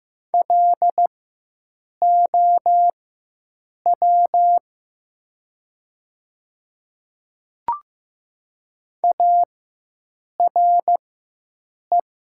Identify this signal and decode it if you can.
Morse code, heard as a keyed tone, about 15 wpm, spaced as for 7 wpm, 700 Hz. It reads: LOW ARE
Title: 4x Character Spacing - Top 500 Words - 15wpm